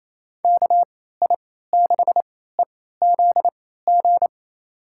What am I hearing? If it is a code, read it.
Morse code, heard as a keyed tone, about 28 wpm, spaced as for 16 wpm, 700 Hz. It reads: KI6EZG
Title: USA Call Signs - 6 Characters - 3x Character Spacing - 28wpm